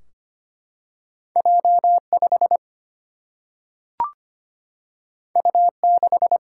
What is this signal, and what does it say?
Morse code, heard as a keyed tone, about 25 wpm, 700 Hz. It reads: J5 U6
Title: Single-Letter Single-Number 25wpm